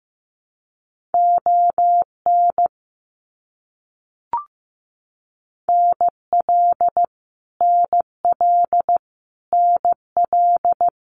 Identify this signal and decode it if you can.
Morse code, heard as a keyed tone, about 15 wpm, 700 Hz. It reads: ON NL NL NL
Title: Canadian Provinces and Territories - Repeated 3x - 15wpm